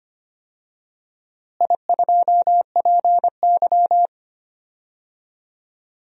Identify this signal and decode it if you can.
Morse code, heard as a keyed tone, about 25 wpm, 700 Hz. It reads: I2PY
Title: DXCC-100 Call Signs - 4 Characters - Repeated 3x - 25wpm